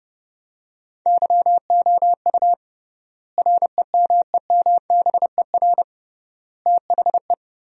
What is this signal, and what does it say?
Morse code, heard as a keyed tone, about 30 wpm, 700 Hz. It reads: YOU REMEMBER THE
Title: 3x Word-Spacing - Sentences from Top 400 Words - 30wpm